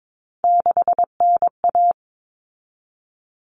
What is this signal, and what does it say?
Morse code, heard as a keyed tone, about 22 wpm, 700 Hz. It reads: 6NA
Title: Common Cut Numbers - 3-Digit 22wpm